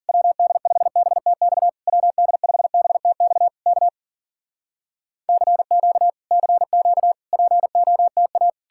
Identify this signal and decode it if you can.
Morse code, heard as a keyed tone, about 60 wpm, spaced as for 52 wpm, 700 Hz. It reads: WD5BTX WD5BTX K CQ CQ POTA